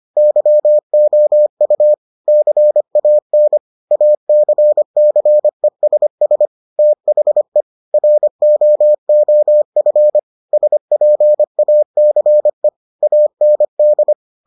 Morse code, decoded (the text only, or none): YOU CAN ACCESS THE ROOF SPACE AND